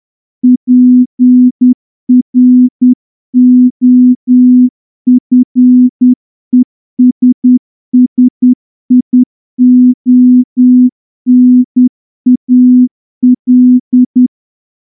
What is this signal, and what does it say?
Morse code, heard as a keyed tone, about 10 wpm, 250 Hz. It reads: PROFESSIONAL
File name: morse-training-10WPM